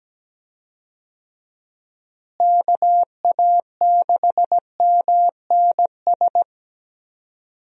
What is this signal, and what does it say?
Morse code, heard as a keyed tone, about 17 wpm, 700 Hz. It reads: KA6MNS
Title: USA Call Signs - 6 Characters - Repeated 3x - 17wpm